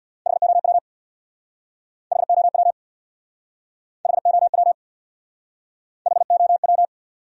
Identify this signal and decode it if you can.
Morse code, heard as a keyed tone, about 60 wpm, 700 Hz. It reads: HOW HOW HOW HOW
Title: Speed-Racing Top 500 Words 50wpm